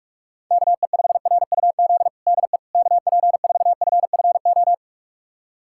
Morse code, heard as a keyed tone, about 46 wpm, 690 Hz. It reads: KE5RUZ DE KP4FFO
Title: USA Callsign DE Callsign - 6 Characters - 45wpm